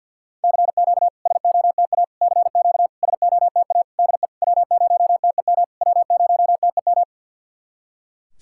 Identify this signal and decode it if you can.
Morse code, heard as a keyed tone, about 47 wpm, 700 Hz. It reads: CQ SOTA CQ SOTA DE W0TEM W0TEM
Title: SOTA QSO Practice 50wpm